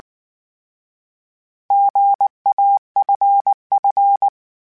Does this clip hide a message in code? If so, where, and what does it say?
Yes: Morse code from 1.7 to 4.3 s, heard as a keyed tone, about 19 wpm, 800 Hz. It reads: GAFF